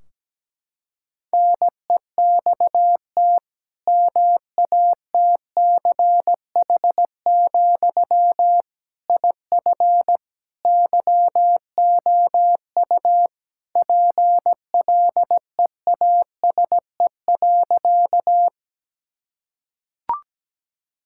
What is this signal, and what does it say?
Morse code, heard as a keyed tone, about 17 wpm, 700 Hz. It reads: NEXT MATCH, IF YOU PLEASE.
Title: Sentences from Top 1600 Words 17wpm